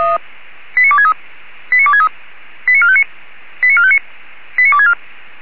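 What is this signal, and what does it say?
Five-tone selective call (ZVEI), heard as quick runs of five tones, tones of about 70 ms. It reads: 89263 89263 89459 89459 89264
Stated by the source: ZVEI five-tone selective call received on an SDR.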